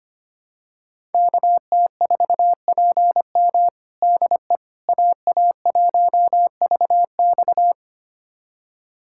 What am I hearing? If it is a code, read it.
Morse code, heard as a keyed tone, about 25 wpm, 700 Hz. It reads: KT4PM DE AA1VX
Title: USA Callsign DE Callsign - 5 Characters - 25wpm